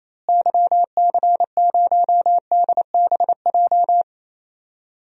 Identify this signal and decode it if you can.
Morse code, heard as a keyed tone, about 28 wpm, 700 Hz. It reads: YC0DBJ
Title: DXCC-100 Call Signs - 6 Characters - Repeated 2x - 28wpm